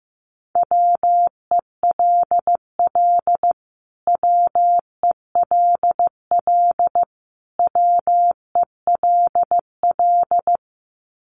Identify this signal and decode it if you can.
Morse code, heard as a keyed tone, about 15 wpm, 700 Hz. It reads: WELL WELL WELL
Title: Top 200 Words Repeated 15wpm